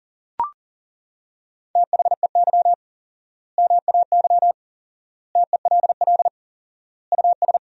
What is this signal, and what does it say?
Morse code, heard as a keyed tone, about 37 wpm, 700 Hz. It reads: THEY MAY TELL US
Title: 4x Word-Spacing - Sentences from Top 300 Words - 40wpm